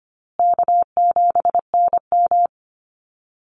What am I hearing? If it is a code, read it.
Morse code, heard as a keyed tone, about 25 wpm, 700 Hz. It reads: K7NM
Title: Call Signs Repeated 2x - 25wpm